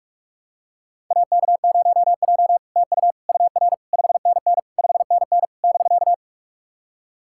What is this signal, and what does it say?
Morse code, heard as a keyed tone, about 48 wpm, 700 Hz. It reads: AK0J TU UR 5NN 5NN <BK>
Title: SOTA QSO Practice 45wpm